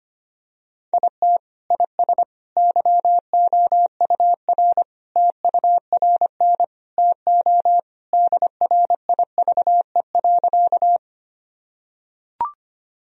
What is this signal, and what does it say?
Morse code, heard as a keyed tone, about 25 wpm, 700 Hz. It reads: IT IS YOUR TURN TO DRIVE.
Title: Sentences from Top 1000 Words - Review Part 2 - 25wpm